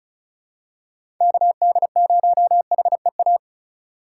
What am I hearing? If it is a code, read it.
Morse code, heard as a keyed tone, about 36 wpm, 700 Hz. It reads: KD0HEA
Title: USA Call Signs - 6 Characters - 2x Character Spacing - 35wpm